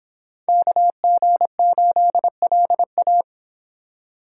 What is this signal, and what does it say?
Morse code, heard as a keyed tone, about 26 wpm, 700 Hz. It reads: KG8LA